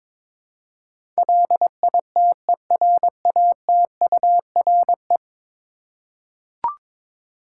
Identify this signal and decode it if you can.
Morse code, heard as a keyed tone, about 21 wpm, 700 Hz. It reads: LITERATURE
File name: practice / 22wpm / 10 Letter Words - First 3 Letters Given - 22wpm